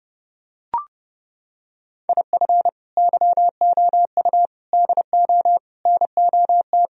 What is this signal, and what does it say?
Morse code, heard as a keyed tone, about 29 wpm, 700 Hz. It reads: IF YOU DO NOT